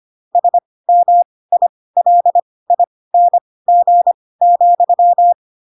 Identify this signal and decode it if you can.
Morse code, heard as a keyed tone, about 25 wpm, spaced as for 18 wpm, 700 Hz. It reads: SMILING,
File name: Pinocchio / Pinocchio ch5 12 25 ik1hge